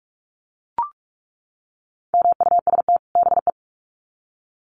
Morse code, heard as a keyed tone, about 44 wpm, 700 Hz. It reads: MUST BE
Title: Sets of 2 Words 45wpm